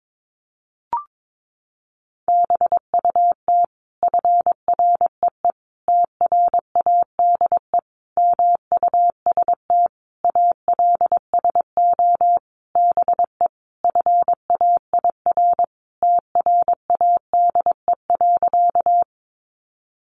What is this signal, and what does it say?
Morse code, heard as a keyed tone, about 22 wpm, 700 Hz. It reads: BUT FREE TRADE MUST ALSO BE FAIR TRADE.
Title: Sentences from Top 1300 Words 22wpm